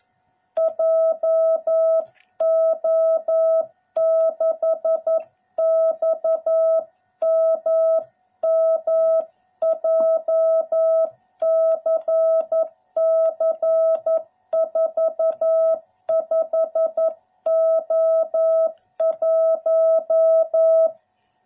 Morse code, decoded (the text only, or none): JO6XMM JCC 45O1